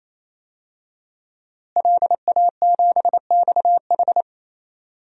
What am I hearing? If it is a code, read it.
Morse code, heard as a keyed tone, about 28 wpm, 700 Hz. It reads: LA7XH
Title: DXCC-100 Call Signs - 5 Characters - Repeated 3x - 28wpm